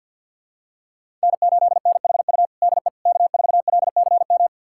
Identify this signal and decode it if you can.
Morse code, heard as a keyed tone, about 51 wpm, 700 Hz. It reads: N8NHU DE K4LCM
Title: USA Callsign DE Callsign - 5 Characters - 50wpm